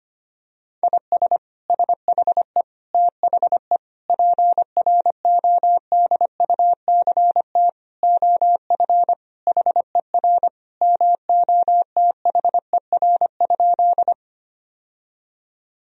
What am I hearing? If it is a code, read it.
Morse code, heard as a keyed tone, about 25 wpm, 700 Hz. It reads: IS SHE THE PRODUCT OF HER MOTHER?
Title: Sentences from Top 1000 Words - Review Part 2 - 25wpm